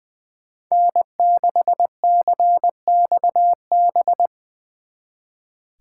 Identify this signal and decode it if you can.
Morse code, heard as a keyed tone, about 25 wpm, 700 Hz. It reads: N6CXB
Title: Speed Racing - USA 1x3 Call Signs 20wpm